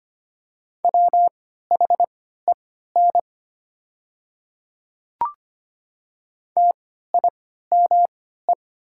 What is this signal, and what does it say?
Morse code, heard as a keyed tone, about 25 wpm, spaced as for 14 wpm, 700 Hz. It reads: WHEN TIME
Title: 3x Character Spacing - Top 100 Words - 25wpm